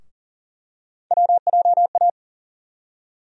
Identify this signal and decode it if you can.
Morse code, heard as a keyed tone, about 40 wpm, 700 Hz. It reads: WJA
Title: USA Call Sign Postfixes - 3 Letters 40wpm